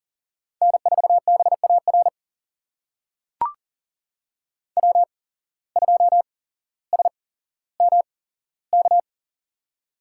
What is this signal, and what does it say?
Morse code, heard as a keyed tone, about 40 wpm, spaced as for 17 wpm, 700 Hz. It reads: N4BAR W2SMK
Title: USA Call Signs - 5 Characters - 8x Character Spacing - 40wpm